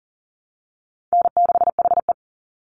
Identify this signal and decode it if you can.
Morse code, heard as a keyed tone, about 40 wpm, 700 Hz. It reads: N6HE
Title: Call Signs Repeated 3x - 40wpm